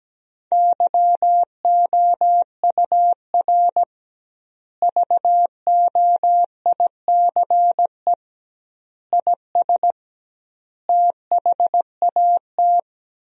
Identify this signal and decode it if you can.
Morse code, heard as a keyed tone, about 17 wpm, 700 Hz. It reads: YOUR VOICE IS THAT
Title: 2x Word-Spacing - Sentences from Top 500 Words - 17wpm